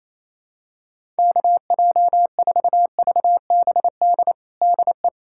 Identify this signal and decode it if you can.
Morse code, heard as a keyed tone, about 27 wpm, 700 Hz. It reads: KJ4VBD DE
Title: USA Callsign DE Callsign 28wpm